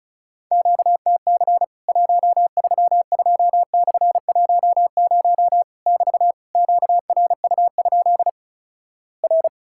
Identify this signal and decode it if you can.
Morse code, heard as a keyed tone, about 36 wpm, 700 Hz. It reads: QTC 132/10 <BT> QRU? R
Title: Worked All Europe DX Contest - QTC Exchange - 35wpm